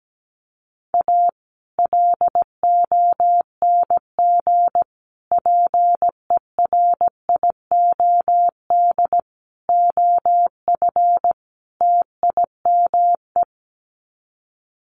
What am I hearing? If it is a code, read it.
Morse code, heard as a keyed tone, about 17 wpm, 700 Hz. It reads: A LONG PERIOD OF TIME
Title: Sets of 5 Words 17wpm